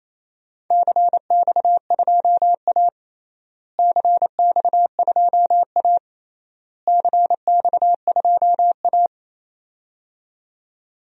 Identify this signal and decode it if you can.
Morse code, heard as a keyed tone, about 28 wpm, 700 Hz. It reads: CX2A CX2A CX2A
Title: DXCC-100 Call Signs - 4 Characters - Repeated 3x - 28wpm